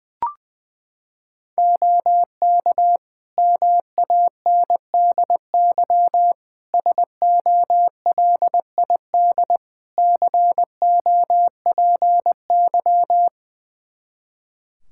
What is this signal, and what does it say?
Morse code, heard as a keyed tone, about 20 wpm, 700 Hz. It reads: OK MANDY SOLID COPY
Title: Practice QSO - Exchange 3 - More Info - 20wpm